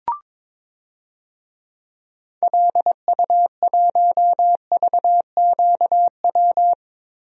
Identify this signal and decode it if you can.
Morse code, heard as a keyed tone, about 22 wpm, 700 Hz. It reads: LU1VQW
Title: DXCC-100 Call Signs - 6 Characters - Repeated 2x - 22wpm